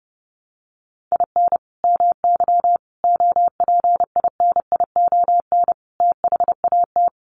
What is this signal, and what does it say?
Morse code, heard as a keyed tone, about 30 wpm, 700 Hz. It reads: IN MY OPINION THAT
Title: Sentences from Top 1200 Words 30wpm